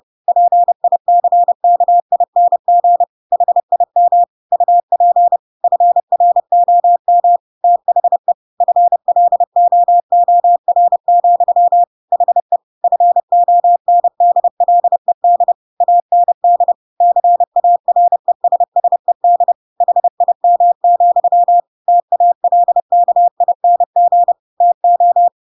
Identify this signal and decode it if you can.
Morse code, heard as a keyed tone, about 30 wpm, 700 Hz. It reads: PICKING HIM UP FROM THE FLOOR, HE FONDLED AND CARESSED HIM, TALKING TO
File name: Pinocchio / Pinocchio ch7 30 30 ik1hge